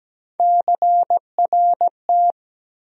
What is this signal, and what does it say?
Morse code, heard as a keyed tone, about 17 wpm, 700 Hz. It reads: CRT
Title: USA Call Sign Postfixes - 3 Letters 17wpm